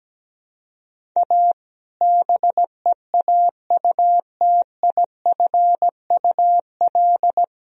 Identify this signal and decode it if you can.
Morse code, heard as a keyed tone, about 17 wpm, 700 Hz. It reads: A BEAUTIFUL
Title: Sentences from Top 1000 Words - Review Part 2 - 17wpm